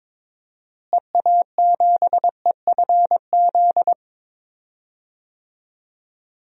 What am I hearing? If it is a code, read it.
Morse code, heard as a keyed tone, about 22 wpm, 700 Hz. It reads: EA7EFZ
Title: DXCC-100 Call Signs - 6 Characters - Repeated 2x - 22wpm